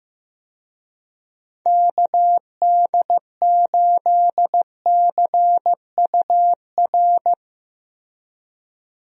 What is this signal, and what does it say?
Morse code, heard as a keyed tone, about 15 wpm, 700 Hz. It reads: KD8CUR
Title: USA Call Signs - 6 Characters - 15wpm